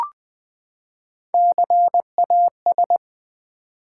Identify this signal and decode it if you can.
Morse code, heard as a keyed tone, about 20 wpm, 700 Hz. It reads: CAS
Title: Build A Word - 3 Letters At A Time - 6-Letter Words - 20wpm